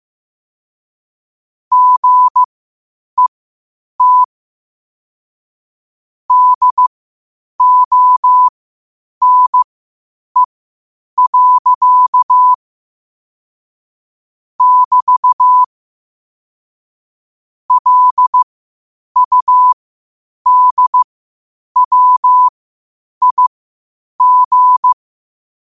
Morse code, heard as a keyed tone, about 15 wpm, spaced as for 8 wpm, 1000 Hz. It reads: GET DONE. <BT> LUDWIG
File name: qotdfiles / 8wpm